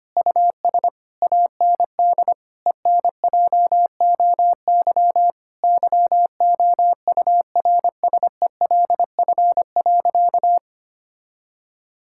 Morse code, heard as a keyed tone, about 25 wpm, 700 Hz. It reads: US AND ENJOY YOURSELF.
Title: Sentences from Top 1000 Words - Review Part 5 - 25wpm